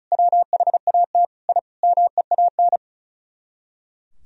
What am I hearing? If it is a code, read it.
Morse code, heard as a keyed tone, about 46 wpm, spaced as for 41 wpm, 700 Hz. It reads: WHAT I MEAN
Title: Speed-Racing - Sets of 5 Words 35wpm